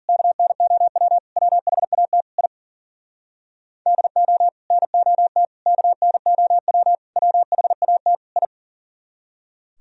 Morse code, heard as a keyed tone, about 52 wpm, 700 Hz. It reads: KNOW WHAT I DO NOT KNOW WHAT I